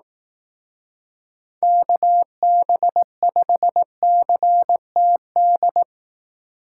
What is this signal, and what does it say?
Morse code, heard as a keyed tone, about 18 wpm, 700 Hz. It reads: KB5CTD
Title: Speed Racing - USA 2x3 Call Signs 15wpm